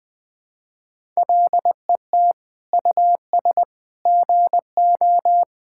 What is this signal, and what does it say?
Morse code, heard as a keyed tone, about 21 wpm, 700 Hz. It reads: LET US GO
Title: Sentences from Top 1000 Words - Review Part 5 - 20wpm